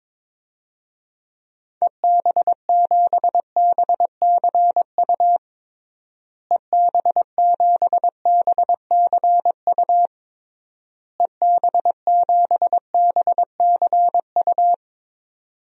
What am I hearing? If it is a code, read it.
Morse code, heard as a keyed tone, about 22 wpm, 700 Hz. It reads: EB7BCU EB7BCU EB7BCU
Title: DXCC-100 Call Signs - 6 Characters - Repeated 3x - 22wpm